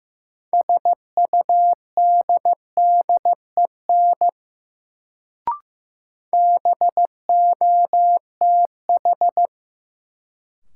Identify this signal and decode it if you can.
Morse code, heard as a keyed tone, about 15 wpm, 700 Hz. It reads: SUDDEN BOTH
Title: Sets of 4 Words - New Word Drill - 15wpm